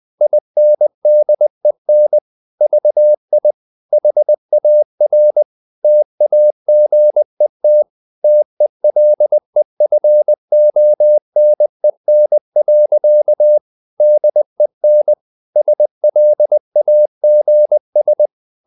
Morse code, heard as a keyed tone, about 20 wpm, 600 Hz. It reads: INDEN VI HAR TAGET TELEFONEN. DEN SLAGS
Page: Ferie 20 wpm